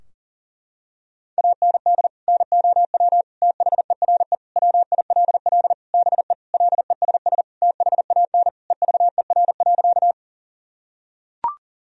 Morse code, heard as a keyed tone, about 40 wpm, 700 Hz. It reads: AND NOW THERE WILL BE LESS THAN EVER.
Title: Sentences from Top 1000 Words - Review Part 2 - 40wpm